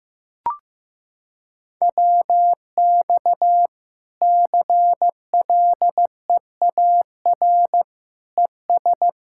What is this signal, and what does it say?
Morse code, heard as a keyed tone, about 15 wpm, 700 Hz. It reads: WX CLEAR ES
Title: Practice QSO - Exchange 2 - Info - 15wpm